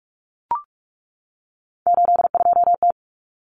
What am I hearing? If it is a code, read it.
Morse code, heard as a keyed tone, about 44 wpm, 700 Hz. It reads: 82T
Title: Common Cut Numbers - 3-Digit 45wpm